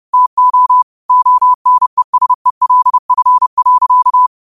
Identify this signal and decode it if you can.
Morse code, heard as a keyed tone, about 30 wpm, 1000 Hz. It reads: TO ONESELF.
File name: qotdfiles / 30wpm